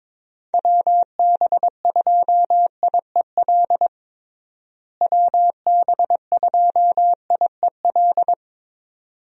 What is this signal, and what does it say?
Morse code, heard as a keyed tone, about 22 wpm, 700 Hz. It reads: WB2IEL WB2IEL
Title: USA Call Signs - 6 Characters - Repeated 2x - 22wpm